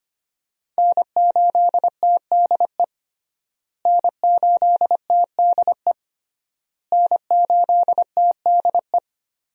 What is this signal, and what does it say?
Morse code, heard as a keyed tone, about 25 wpm, 700 Hz. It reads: N8TDE N8TDE N8TDE
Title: USA Call Signs - 5 Characters - Repeated 3x - 25wpm